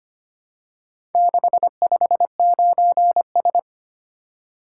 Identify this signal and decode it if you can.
Morse code, heard as a keyed tone, about 25 wpm, 700 Hz. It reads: 659S
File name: practice / 25wpm / Four Letter-Number Combinations - 25wpm